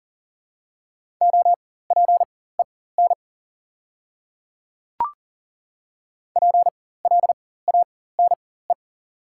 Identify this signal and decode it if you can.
Morse code, heard as a keyed tone, about 40 wpm, spaced as for 19 wpm, 700 Hz. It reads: OPEN PLANE
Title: 4x Character Spacing - Top 500 Words - 40wpm